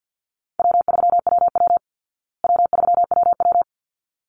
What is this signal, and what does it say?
Morse code, heard as a keyed tone, about 50 wpm, 700 Hz. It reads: W3WW W3WW
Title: Easy Call Signs Repeated 2x - 50wpm